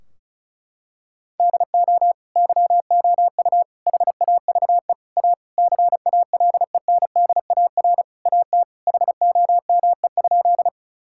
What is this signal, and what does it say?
Morse code, heard as a keyed tone, about 36 wpm, 700 Hz. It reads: DO YOU HAVE A CALENDAR AT HOME?
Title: Sentences from Top 1600 Words 35wpm